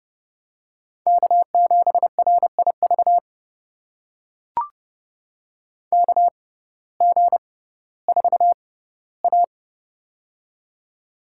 Morse code, heard as a keyed tone, about 30 wpm, spaced as for 17 wpm, 700 Hz. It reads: K7RIV KG4A
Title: USA Call Signs - 6x Character Spacing - 30wpm